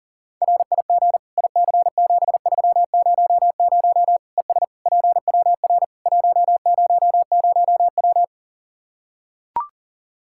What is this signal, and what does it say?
Morse code, heard as a keyed tone, about 40 wpm, 700 Hz. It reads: RIG IC7300 ES PWR 100W
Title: Practice QSO - Exchange 2 - Info - 40wpm